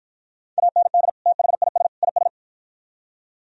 Morse code, heard as a keyed tone, about 48 wpm, 700 Hz. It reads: AND THIS IS